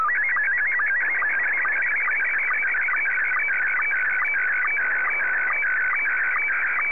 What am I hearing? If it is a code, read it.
SSTV (slow-scan television), Scottie 1.